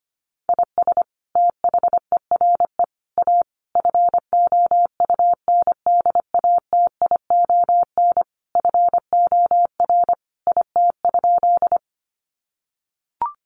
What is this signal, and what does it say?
Morse code, heard as a keyed tone, about 25 wpm, 700 Hz. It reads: IS THERE A FOUNDATION FOR IT?
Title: Sentences from Top 1200 Words 25wpm